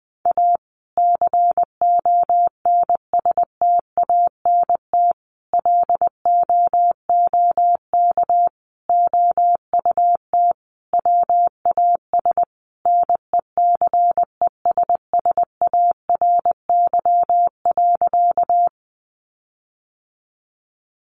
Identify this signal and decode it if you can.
Morse code, heard as a keyed tone, about 20 wpm, 700 Hz. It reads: A CONSTANT LOOK OUT WAS NECESSARY.
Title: Sentences from Top 1500 Words 20wpm